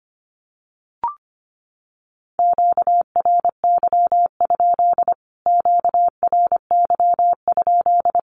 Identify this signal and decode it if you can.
Morse code, heard as a keyed tone, about 25 wpm, 700 Hz. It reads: QRY? QRY?
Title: Q-Codes Repeated 3x 25wpm